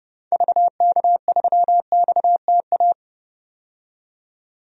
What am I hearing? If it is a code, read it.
Morse code, heard as a keyed tone, about 31 wpm, 700 Hz. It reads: VK3XTA
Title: DXCC-100 Call Signs - 6 Characters - 30wpm